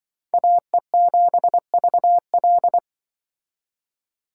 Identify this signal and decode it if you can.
Morse code, heard as a keyed tone, about 24 wpm, 700 Hz. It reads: AE7VL
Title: Speed Racing - USA 2x2 Call Signs 20wpm